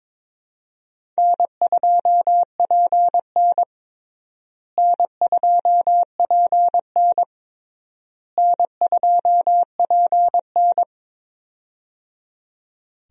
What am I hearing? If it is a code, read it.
Morse code, heard as a keyed tone, about 22 wpm, 700 Hz. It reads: N2PN N2PN N2PN
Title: USA Call Signs - 4 Characters - Repeated 3x - 22wpm